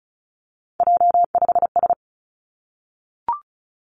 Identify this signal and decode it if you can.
Morse code, heard as a keyed tone, about 36 wpm, 700 Hz. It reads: J5S
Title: Three Challenging Letter-Number Combinations 35wpm